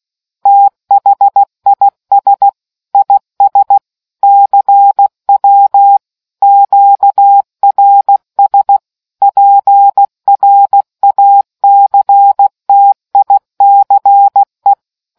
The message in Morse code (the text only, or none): THIS IS CW QRS PRACTICE